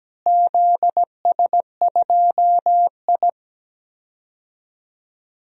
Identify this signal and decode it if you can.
Morse code, heard as a keyed tone, about 17 wpm, 700 Hz. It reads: ZS2I
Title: DXCC-100 Call Signs - 4 Characters - Repeated 3x - 17wpm